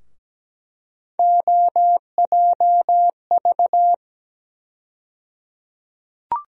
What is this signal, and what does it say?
Morse code, heard as a keyed tone, about 17 wpm, 700 Hz. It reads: OJV
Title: USA Call Sign Postfixes - 3 Letters 17wpm